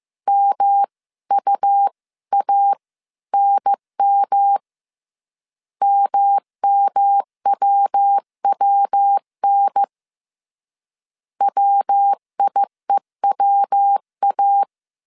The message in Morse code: MUANM MMWWN WIEWA

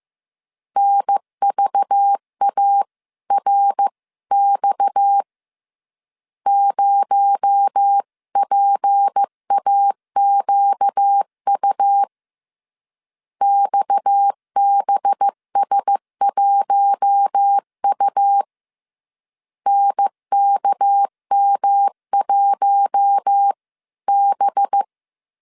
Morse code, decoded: NVARX 0PAQU XBS1U NKM1B

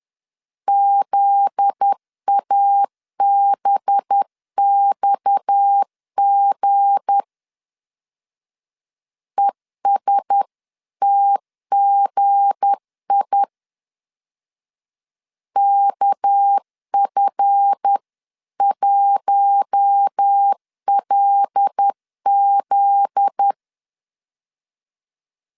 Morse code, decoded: ZABXG ESTGI KF1LZ

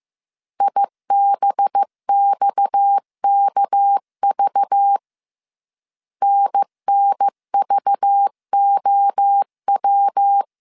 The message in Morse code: IBXKV NNVOW

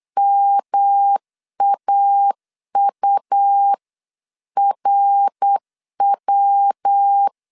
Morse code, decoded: MAURW

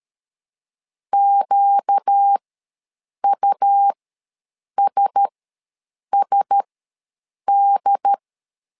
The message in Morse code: QUSSD

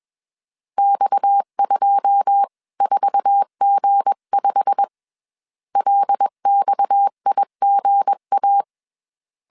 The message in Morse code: X24G5 LXIGA

